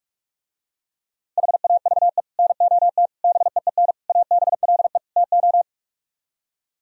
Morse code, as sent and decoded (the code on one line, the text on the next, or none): .... .- ...- . / -. --- - / -... . -. / .- -... .-.. / - ---
HAVE NOT BEEN ABLE TO